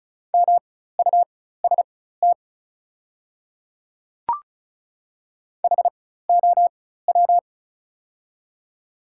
Morse code, heard as a keyed tone, about 36 wpm, spaced as for 16 wpm, 690 Hz. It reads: MUST HOW
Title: 4x Character Spacing - Top 200 Words - 35wpm